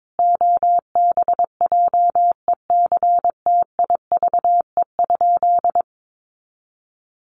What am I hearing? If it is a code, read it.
Morse code, heard as a keyed tone, about 22 wpm, 700 Hz. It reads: OBJECTIVE?
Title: Sentences from Top 1200 Words 22wpm